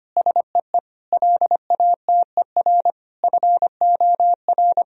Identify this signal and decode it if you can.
Morse code, heard as a keyed tone, about 25 wpm, 700 Hz. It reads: SEE LATER FOR
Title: Sentences from Top 1000 Words - Review Part 4 - 25wpm